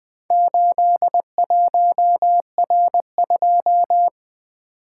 Morse code, heard as a keyed tone, about 20 wpm, 700 Hz. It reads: 81R2
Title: Four Letter-Number Combinations - 20wpm